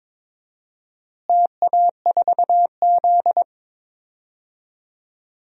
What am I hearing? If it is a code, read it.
Morse code, heard as a keyed tone, about 22 wpm, 700 Hz. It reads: TA4Z
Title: DXCC-100 Call Signs - 4 Characters - Repeated 2x - 22wpm